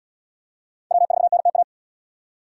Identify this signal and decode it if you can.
Morse code, heard as a keyed tone, about 74 wpm, 700 Hz. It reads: W4NTA